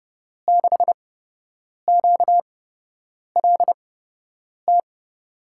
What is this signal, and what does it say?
Morse code, heard as a keyed tone, about 32 wpm, spaced as for 8 wpm, 700 Hz. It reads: 6QLT
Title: USA Call Signs - 6 Characters - 8x Character Spacing - 30wpm